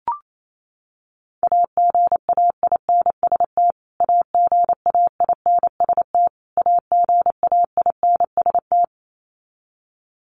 Morse code, heard as a keyed tone, about 28 wpm, 700 Hz. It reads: AGAINST AGAINST AGAINST
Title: Top 500 Words Repeated 28wpm